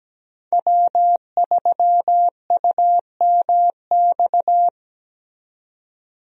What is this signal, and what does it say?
Morse code, heard as a keyed tone, about 17 wpm, 700 Hz. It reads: W3UMX
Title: USA Call Signs - 5 Characters - 17wpm